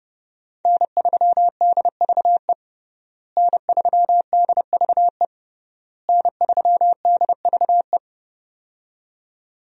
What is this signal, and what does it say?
Morse code, heard as a keyed tone, about 32 wpm, 700 Hz. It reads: N3DVE N3DVE N3DVE
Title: USA Call Signs - 5 Characters - Repeated 3x - 30wpm